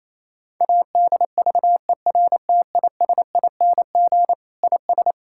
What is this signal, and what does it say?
Morse code, heard as a keyed tone, about 28 wpm, 700 Hz. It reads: ADVERTISING IS
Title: Sentences from Top 1000 Words - Review Part 6 - 28wpm